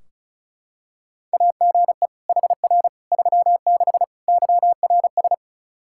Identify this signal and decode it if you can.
Morse code, heard as a keyed tone, about 34 wpm, 700 Hz. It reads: AGE HR 36 YRS
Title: Practice QSO - Exchange 3 - More Info - 35wpm